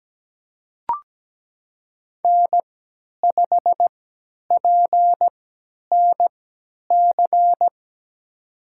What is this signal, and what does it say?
Morse code, heard as a keyed tone, about 17 wpm, spaced as for 10 wpm, 700 Hz. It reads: N5PNC
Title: USA Call Signs - 5 Characters - 3x Character Spacing - 17wpm